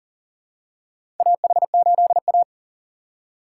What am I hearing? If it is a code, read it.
Morse code, heard as a keyed tone, about 40 wpm, 700 Hz. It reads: AH8A
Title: USA Call Signs - 4 Characters - 6x Character Spacing - 40wpm